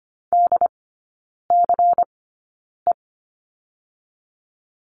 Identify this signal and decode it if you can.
Morse code, heard as a keyed tone, about 25 wpm, spaced as for 9 wpm, 700 Hz. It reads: DCE
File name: Lesson 17e - Letter u - Three Characters 25wpm